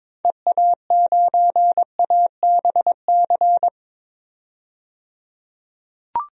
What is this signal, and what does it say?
Morse code, heard as a keyed tone, about 22 wpm, 700 Hz. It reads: EA9ABC
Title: DXCC-100 Call Signs - 6 Characters - 22wpm